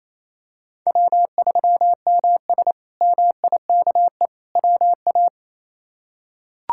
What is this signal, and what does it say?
Morse code, heard as a keyed tone, about 28 wpm, 700 Hz. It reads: W3MS MIKE WA
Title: K1USN SST QSO Practice - 28wpm